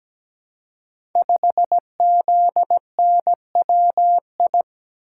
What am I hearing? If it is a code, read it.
Morse code, heard as a keyed tone, about 17 wpm, 700 Hz. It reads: 5ZNWI
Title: Code Groups - 5 Letter:Numbers - 17wpm